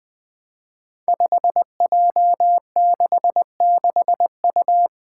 Morse code, heard as a keyed tone, about 20 wpm, 700 Hz. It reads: 5J66U
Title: Five Challenging Letter-Number Combinations 20wpm